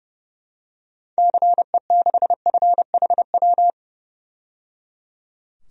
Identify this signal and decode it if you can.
Morse code, heard as a keyed tone, about 30 wpm, 700 Hz. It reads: CE6FHW
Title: DXCC-100 Call Signs - 6 Characters - 30wpm